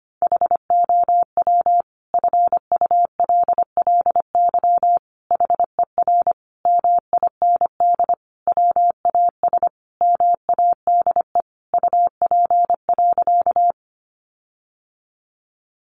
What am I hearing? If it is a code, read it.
Morse code, heard as a keyed tone, about 25 wpm, 700 Hz. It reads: HOW FULLY HER MIND WAS MADE UP.
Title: Sentences from Top 1300 Words 25wpm